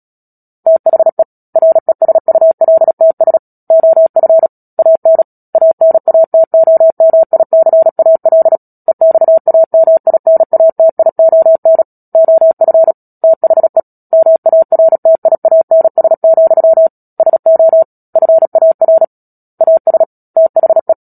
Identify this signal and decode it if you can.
Morse code, heard as a keyed tone, about 37 wpm, 650 Hz. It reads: THE RESULTS OF AN ANATOMICAL EXAMINATION OF THE MARTIANS, SO FAR AS THE
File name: Book Two / 10-Chapter 10 - Epilogue - 36 WPM